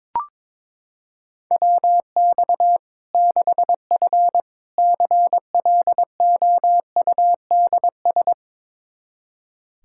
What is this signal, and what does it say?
Morse code, heard as a keyed tone, about 22 wpm, 700 Hz. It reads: WX 6F CLOUDS
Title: Practice QSO - Exchange 2 - Info - 22wpm